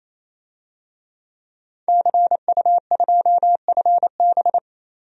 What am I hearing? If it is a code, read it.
Morse code, heard as a keyed tone, about 28 wpm, 700 Hz. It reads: CU2FB
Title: DXCC-100 Call Signs - 5 Characters - 28wpm